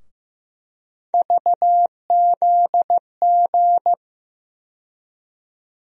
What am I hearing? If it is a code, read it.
Morse code, heard as a keyed tone, about 15 wpm, 700 Hz. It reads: VZG
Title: USA Call Sign Postfixes - 3 Letters 15wpm